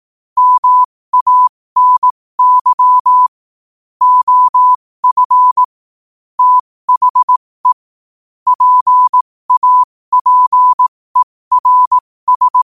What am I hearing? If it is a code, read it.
Morse code, heard as a keyed tone, about 18 wpm, spaced as for 15 wpm, 1000 Hz. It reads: MANY OF THE PAPERS
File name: qotdfiles / 15wpm